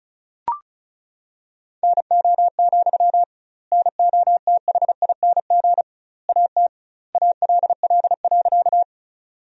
Morse code, heard as a keyed tone, about 34 wpm, 700 Hz. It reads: NO, NOTHING AT ALL.
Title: 2x Word-Spacing - Sentences from Top 500 Words - 35wpm